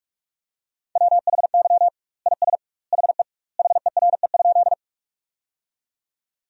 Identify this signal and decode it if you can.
Morse code, heard as a keyed tone, about 42 wpm, 700 Hz. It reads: WHY IS HE HERE?
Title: 2x Word-Spacing - Sentences from Top 200 Words - 45wpm